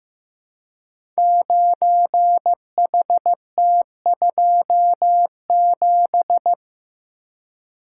Morse code, heard as a keyed tone, about 15 wpm, 700 Hz. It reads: 9HT27
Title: Code Groups - 5 Letter:Numbers - 15wpm